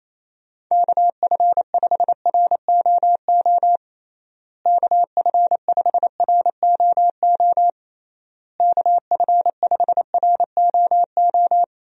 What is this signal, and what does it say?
Morse code, heard as a keyed tone, about 28 wpm, 700 Hz. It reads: KF5ROO KF5ROO KF5ROO
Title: USA Call Signs - 6 Characters - Repeated 3x - 28wpm